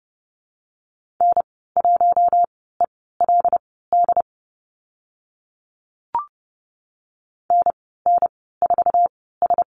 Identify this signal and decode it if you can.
Morse code, heard as a keyed tone, about 30 wpm, spaced as for 17 wpm, 700 Hz. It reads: N1ELD NN4S
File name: Lesson 15g - Number 5 - Call Signs 30wpm